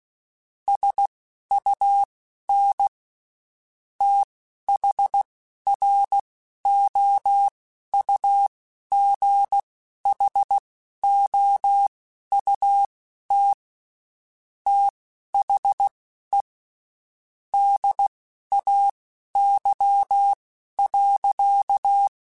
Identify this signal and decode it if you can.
Morse code, heard as a keyed tone, about 16 wpm, spaced as for 12 wpm, 800 Hz. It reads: SUN THROUGHOUT THE DAY.
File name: Spaced Characters / Spaced Characters-mp3s / Weather 5 12 sp @15